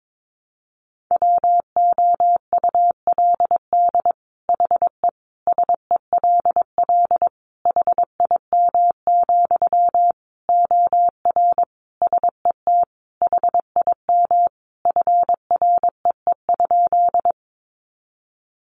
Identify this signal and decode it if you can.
Morse code, heard as a keyed tone, about 22 wpm, 700 Hz. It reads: WOULD HE SELL HIM, OR SET HIM FREE?
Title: Sentences from Top 1100 Words 22wpm